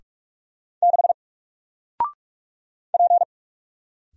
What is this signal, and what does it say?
Morse code, heard as a keyed tone, about 48 wpm, 700 Hz. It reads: 6 P
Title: Single Character (Letter:Number:Punctuation:Prosign) Alt 45wpm